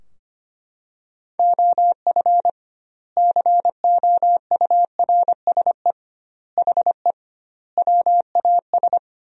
Morse code, heard as a keyed tone, about 25 wpm, 700 Hz. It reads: OF COURSE HE WAS
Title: 2x Word-Spacing - Sentences from Top 500 Words - 25wpm